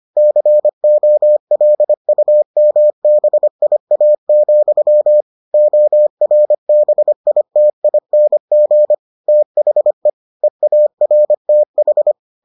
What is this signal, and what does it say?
Morse code, heard as a keyed tone, about 25 wpm, 600 Hz. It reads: COLUMBIA, ORBITING THE EARTH